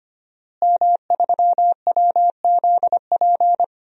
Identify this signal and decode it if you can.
Morse code, heard as a keyed tone, about 25 wpm, 700 Hz. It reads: M3WZP
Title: DXCC-100 Call Signs - 5 Characters - Repeated 2x - 25wpm